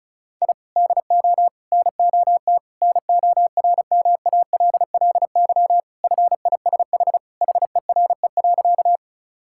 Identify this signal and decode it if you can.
Morse code, heard as a keyed tone, about 36 wpm, 700 Hz. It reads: I DO NOT NORMALLY FISH HERE.
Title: Sentences from Top 1000 Words - Review Part 6 - 35wpm